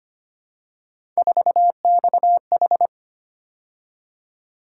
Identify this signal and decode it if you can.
Morse code, heard as a keyed tone, about 25 wpm, 700 Hz. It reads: 4XH
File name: Three Letter-Number Combinations - 25wpm